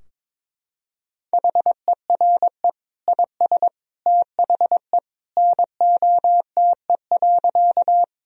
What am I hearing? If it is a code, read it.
Morse code, heard as a keyed tone, about 22 wpm, 700 Hz. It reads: HERE IS THE NOTE.
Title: Sentences from Top 1000 Words - Review Part 2 - 22wpm